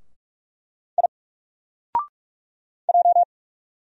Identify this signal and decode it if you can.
Morse code, heard as a keyed tone, about 44 wpm, 700 Hz. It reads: I J
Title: Single Character (Letter:Number:Punctuation:Prosign) Alt 45wpm